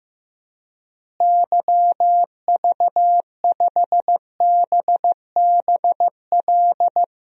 Morse code, heard as a keyed tone, about 15 wpm, 700 Hz. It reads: YV5BBL
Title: DXCC-100 Call Signs - 6 Characters - Repeated 3x - 15wpm